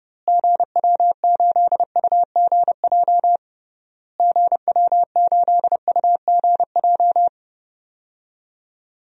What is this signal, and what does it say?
Morse code, heard as a keyed tone, about 30 wpm, 700 Hz. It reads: GW8UGJ GW8UGJ
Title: DXCC-100 Call Signs - 6 Characters - Repeated 3x - 30wpm